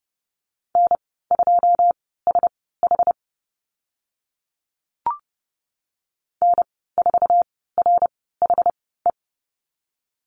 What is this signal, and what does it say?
Morse code, heard as a keyed tone, about 30 wpm, spaced as for 17 wpm, 700 Hz. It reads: N2SH N4RHE
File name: Lesson 14g - Number 2 - Call Signs 30wpm